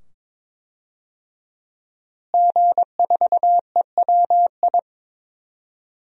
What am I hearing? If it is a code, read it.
Morse code, heard as a keyed tone, about 22 wpm, 700 Hz. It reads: G4EWI
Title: DXCC-100 Call Signs - 5 Characters - Repeated 2x - 22wpm